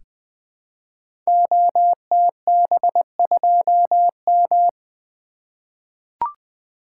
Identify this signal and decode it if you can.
Morse code, heard as a keyed tone, about 20 wpm, 700 Hz. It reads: OTB2M
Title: Code Groups - 5 Letter:Numbers - 20wpm